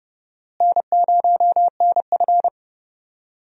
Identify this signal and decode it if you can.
Morse code, heard as a keyed tone, about 32 wpm, 700 Hz. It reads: N0NF
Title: USA Call Signs - 4 Characters - Repeated 3x - 30wpm